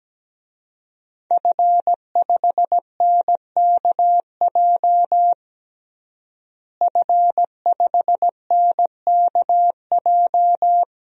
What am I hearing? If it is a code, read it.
Morse code, heard as a keyed tone, about 17 wpm, 700 Hz. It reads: F5NKJ F5NKJ
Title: DXCC-100 Call Signs - 5 Characters - Repeated 3x - 17wpm